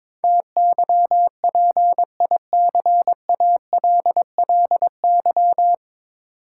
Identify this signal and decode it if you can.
Morse code, heard as a keyed tone, about 22 wpm, 700 Hz. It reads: TYPICALLY
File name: practice / 22wpm / Nine Letter Words - 22wpm